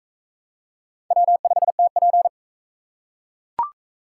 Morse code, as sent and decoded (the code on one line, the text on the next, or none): .-- ..... - .--.
W5TP